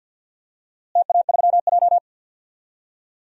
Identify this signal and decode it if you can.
Morse code, heard as a keyed tone, about 49 wpm, 700 Hz. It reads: TA3J
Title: DXCC-100 Call Signs - 4 Characters - Repeated 3x - 50wpm